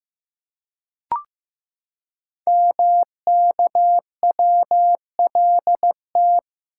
Morse code, heard as a keyed tone, about 15 wpm, 700 Hz. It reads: MKWLT
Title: Code Groups - 5 Letter:Numbers - 15wpm